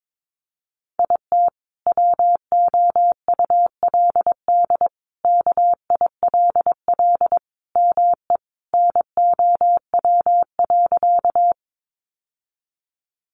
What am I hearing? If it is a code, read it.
Morse code, heard as a keyed tone, about 22 wpm, 700 Hz. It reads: IT WOULD KILL ME NOW.
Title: Sentences from Top 1400 Words 22wpm